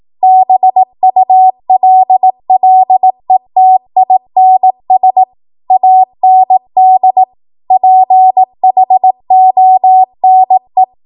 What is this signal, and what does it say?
Morse code, heard as a keyed tone, about 18 wpm, 750 Hz. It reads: BULLETINS AND PHONE
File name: Archive / 18 WPM / 18WPM